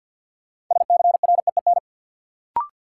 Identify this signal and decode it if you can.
Morse code, heard as a keyed tone, about 53 wpm, 700 Hz. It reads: SCREEN
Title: Sentences from Top 1000 Words - Review Part 6 - New Word Drill - 50wpm